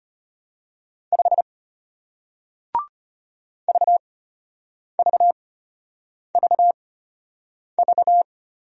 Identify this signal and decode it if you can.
Morse code, heard as a keyed tone, about 34 wpm, spaced as for 8 wpm, 690 Hz. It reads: 5 VVVV